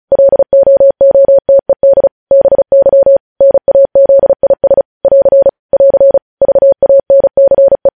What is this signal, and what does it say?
Morse code, heard as a keyed tone, about 36 wpm, 550 Hz. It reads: LOOTED BY NAZIS <AR> <AR> VANCE
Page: News headlines in Morse code at 35 wpm